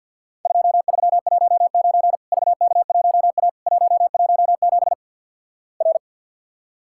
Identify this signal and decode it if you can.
Morse code, heard as a keyed tone, about 48 wpm, 700 Hz. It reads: 2319 VK1A 117 R
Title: Worked All Europe DX Contest - QTC Exchange - 50wpm